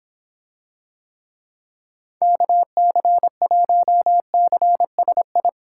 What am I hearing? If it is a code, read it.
Morse code, heard as a keyed tone, about 26 wpm, 700 Hz. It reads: KC1CSI